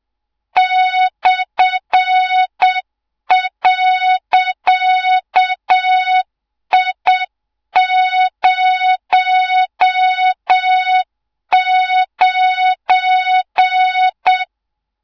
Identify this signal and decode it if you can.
Morse code, heard as a keyed tone, about 7 wpm, 3700 Hz, with noise. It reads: /.I09
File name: Morze-HA2MN / hang / 24 GYAKORLAT